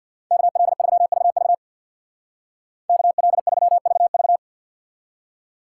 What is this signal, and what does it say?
Morse code, heard as a keyed tone, about 59 wpm, 690 Hz. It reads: KL3VV KL3VV
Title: Speed Racing - USA 2x2 Call Signs 50wpm